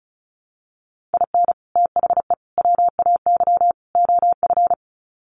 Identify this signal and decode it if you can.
Morse code, heard as a keyed tone, about 34 wpm, 700 Hz. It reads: IN THE WAY OF
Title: Sets of 4 Words 35wpm